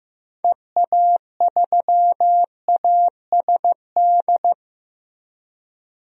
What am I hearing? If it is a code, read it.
Morse code, heard as a keyed tone, about 15 wpm, 700 Hz. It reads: EA3ASD
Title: DXCC-100 Call Signs - 6 Characters - Repeated 3x - 15wpm